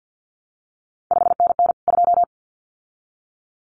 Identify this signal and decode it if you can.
Morse code, heard as a keyed tone, about 50 wpm, 700 Hz. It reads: 5NN 2
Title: CQ WW DX Contest Exchange - 50wpm